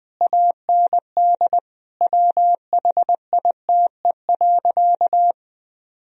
Morse code, heard as a keyed tone, about 20 wpm, 700 Hz. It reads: AND WHITE.
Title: Sentences from Top 1000 Words - Review Part 2 - 20wpm